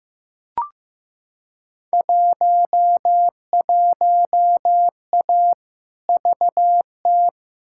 Morse code, heard as a keyed tone, about 15 wpm, 700 Hz. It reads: 11A VT
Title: Field Day Exchanges Encore - 15wpm